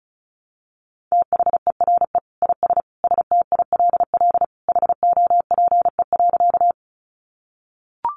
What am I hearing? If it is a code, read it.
Morse code, heard as a keyed tone, about 34 wpm, 700 Hz. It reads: THERE IS STILL HOPE.
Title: Sentences from Top 1000 Words 35wpm